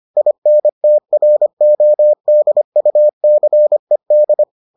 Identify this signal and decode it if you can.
Morse code, heard as a keyed tone, about 25 wpm, 600 Hz. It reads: INTRODUCED